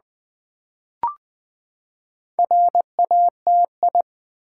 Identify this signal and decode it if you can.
Morse code, heard as a keyed tone, about 20 wpm, 700 Hz. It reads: RATI
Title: Build A Word - 4 Letters At A Time - 8-Letter Words - 20wpm